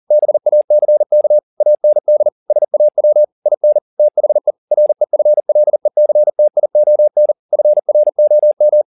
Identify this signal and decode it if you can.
Morse code, heard as a keyed tone, about 40 wpm, 600 Hz. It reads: BACK AND SAW IN THE REFLECTION FROM
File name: the old man and the sea 2 40wpm